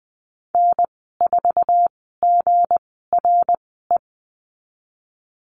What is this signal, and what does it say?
Morse code, heard as a keyed tone, about 20 wpm, spaced as for 15 wpm, 700 Hz. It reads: N4GRE
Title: 161 Lesson 26g - Letter g - Call Signs 20wpm